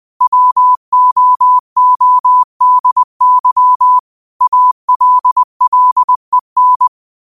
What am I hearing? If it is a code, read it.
Morse code, heard as a keyed tone, about 20 wpm, 1000 Hz. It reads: WOODY ALLEN
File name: qotdfiles / 20wpm